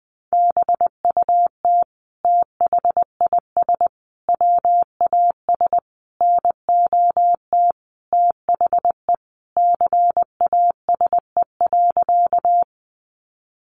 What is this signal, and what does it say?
Morse code, heard as a keyed tone, about 20 wpm, 700 Hz. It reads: BUT THIS WAS NOT THE CASE.
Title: Sentences from Top 700 Words 20wpm